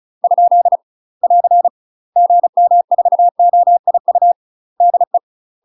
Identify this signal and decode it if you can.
Morse code, heard as a keyed tone, about 34 wpm, 700 Hz. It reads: ? <AR> GM4OIU DE
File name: UK MASTER QSOs TRAINING/35WPM